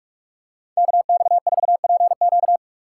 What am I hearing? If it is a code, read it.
Morse code, heard as a keyed tone, about 46 wpm, 700 Hz. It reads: KX4PQ
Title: K1USN SST QSO Practice - 45wpm